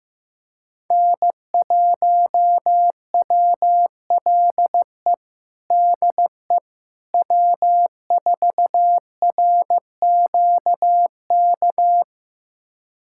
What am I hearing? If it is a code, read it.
Morse code, heard as a keyed tone, about 15 wpm, 700 Hz. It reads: N1WLE DE W4RQK
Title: USA Callsign DE Callsign - 5 Characters - 15wpm